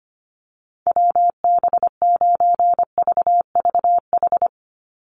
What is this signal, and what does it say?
Morse code, heard as a keyed tone, about 25 wpm, 690 Hz. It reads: WB9VVH
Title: USA Call Signs 25wpm